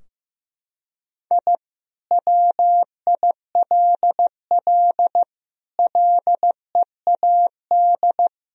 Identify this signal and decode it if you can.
Morse code, heard as a keyed tone, about 15 wpm, 700 Hz. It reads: I WILL LEAD
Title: Sentences from Top 1000 Words - Review Part 2 - 15wpm